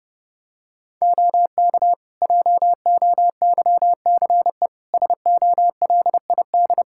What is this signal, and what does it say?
Morse code, heard as a keyed tone, about 32 wpm, 700 Hz. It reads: OK JOYCE SOLID
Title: Practice QSO - Exchange 3 - More Info - 30wpm